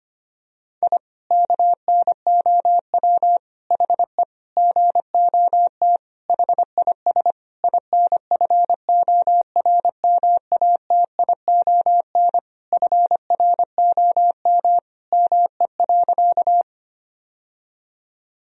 Morse code, heard as a keyed tone, about 25 wpm, 700 Hz. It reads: I KNOW HE GOT HIS INFORMATION FROM ME.
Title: Sentences from Top 1000 Words - Review Part 3 - 25wpm